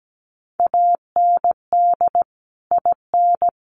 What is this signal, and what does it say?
Morse code, heard as a keyed tone, about 17 wpm, 700 Hz. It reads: AND IN
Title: Sets of 2 Words 17wpm